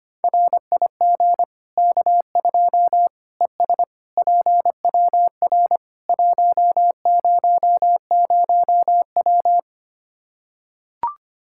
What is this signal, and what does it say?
Morse code, heard as a keyed tone, about 25 wpm, 700 Hz. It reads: RIG K2 ES PWR 100W
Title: Practice QSO - Exchange 2 - Info - 25wpm